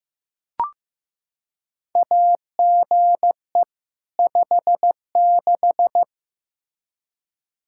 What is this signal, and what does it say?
Morse code, heard as a keyed tone, about 15 wpm, 700 Hz. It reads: AGE 56
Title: K1USN SST QSO Practice 15wpm